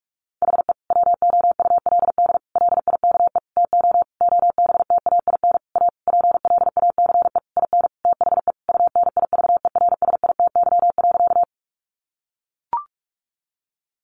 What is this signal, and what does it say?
Morse code, heard as a keyed tone, about 44 wpm, 700 Hz. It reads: HE WOULD LIKE TO OBTAIN A PLACE IN THE UNIVERSITY.
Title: Sentences from Top 1300 Words 45wpm